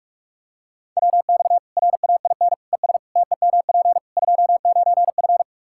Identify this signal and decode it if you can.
Morse code, heard as a keyed tone, about 46 wpm, 700 Hz. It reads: WX RAIN ES TEMP 29F
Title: Practice QSO - Exchange 2 - Info - 45wpm